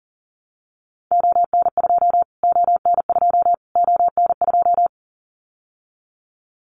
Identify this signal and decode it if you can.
Morse code, heard as a keyed tone, about 40 wpm, 700 Hz. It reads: ON2 ON2 ON2
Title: Callsign Prefixes for Top 25 DXCC - Repeated 3x - 40wpm